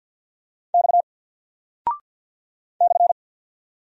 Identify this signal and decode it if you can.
Morse code, heard as a keyed tone, about 48 wpm, 700 Hz. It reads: X /
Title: Single Character (Letter:Number:Punctuation:Prosign) 50wpm